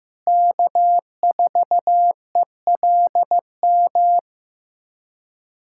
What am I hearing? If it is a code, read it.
Morse code, heard as a keyed tone, about 15 wpm, 700 Hz. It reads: K4ELM
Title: USA Callsign DE Callsign - 5 Characters - 15wpm